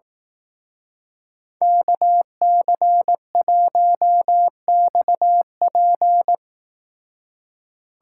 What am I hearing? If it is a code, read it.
Morse code, heard as a keyed tone, about 18 wpm, 700 Hz. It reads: KC1XP
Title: Speed Racing - USA Call Signs 15wpm